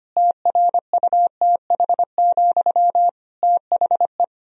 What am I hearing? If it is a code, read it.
Morse code, heard as a keyed tone, about 25 wpm, 700 Hz. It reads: TRUTH, THE
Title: Sentences from Top 1000 Words - Review Part 2 - 25wpm